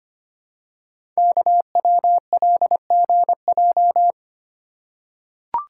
Morse code, heard as a keyed tone, about 25 wpm, 700 Hz. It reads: KWLGJ
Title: Code Groups - 5 Letters - 25wpm